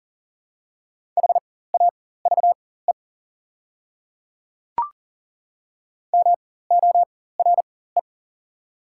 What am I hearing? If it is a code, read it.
Morse code, heard as a keyed tone, about 40 wpm, spaced as for 19 wpm, 690 Hz. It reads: HAVE MORE
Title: 4x Character Spacing - Top 100 Words - 40wpm